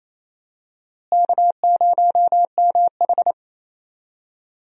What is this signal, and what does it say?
Morse code, heard as a keyed tone, about 28 wpm, 700 Hz. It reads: K0MH
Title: USA Call Signs - 4 Characters - Repeated 3x - 28wpm